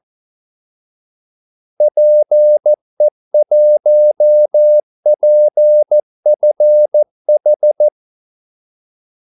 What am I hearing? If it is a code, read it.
Morse code, heard as a keyed tone, about 14 wpm, 600 Hz. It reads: PE1PFH
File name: Callsigns-14wpm